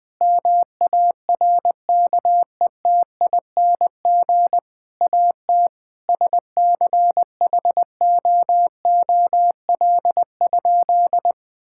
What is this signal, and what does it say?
Morse code, heard as a keyed tone, about 20 wpm, 700 Hz. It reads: MARKETING AT SCHOOL?
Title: Sentences from Top 1000 Words - Review Part 5 - 20wpm